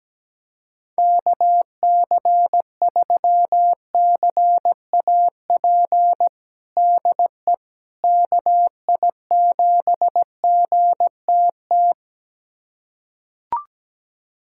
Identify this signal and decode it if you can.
Morse code, heard as a keyed tone, about 17 wpm, 700 Hz. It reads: KC3CAP DE KI7GTT
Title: USA Callsign DE Callsign 17wpm